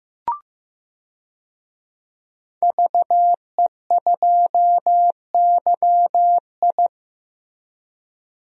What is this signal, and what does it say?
Morse code, heard as a keyed tone, about 15 wpm, 700 Hz. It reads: VE2YI
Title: DXCC-100 Call Signs - 5 Characters - Repeated 3x - 15wpm